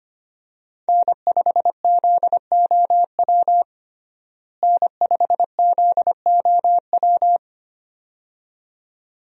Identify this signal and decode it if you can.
Morse code, heard as a keyed tone, about 25 wpm, 700 Hz. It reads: N5ZOW N5ZOW
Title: USA Call Signs - 5 Characters - Repeated 2x - 25wpm